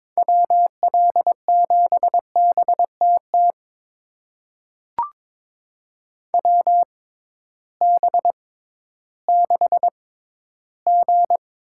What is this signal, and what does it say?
Morse code, heard as a keyed tone, about 22 wpm, spaced as for 13 wpm, 700 Hz. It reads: WL7BTT WB6G
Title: USA Call Signs - 6 Characters - 6x Character Spacing - 22wpm